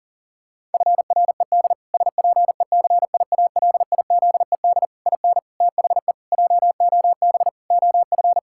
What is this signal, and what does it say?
Morse code, heard as a keyed tone, about 40 wpm, 700 Hz. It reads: FRED SPECIALIZED IN THE JOB OF
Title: Pangram Challenge v2 - 40wpm